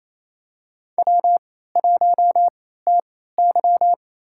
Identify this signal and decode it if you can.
Morse code, heard as a keyed tone, about 28 wpm, spaced as for 16 wpm, 700 Hz. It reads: W1TY
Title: USA Call Signs - 3x Character Spacing - 28wpm